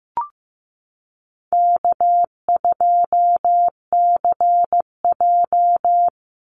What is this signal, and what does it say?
Morse code, heard as a keyed tone, about 15 wpm, 700 Hz. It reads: K2CJ
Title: Easy Call Signs - 15wpm